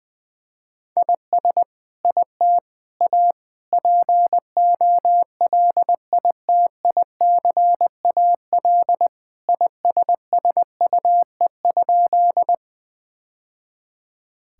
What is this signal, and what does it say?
Morse code, heard as a keyed tone, about 20 wpm, 700 Hz. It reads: IS IT A POLITICAL ISSUE?
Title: Sentences from Top 1000 Words - Review Part 6 - 20wpm